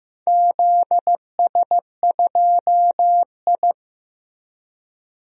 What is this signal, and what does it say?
Morse code, heard as a keyed tone, about 15 wpm, 700 Hz. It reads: ZS2I
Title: DXCC-100 Call Signs - 4 Characters - Repeated 3x - 15wpm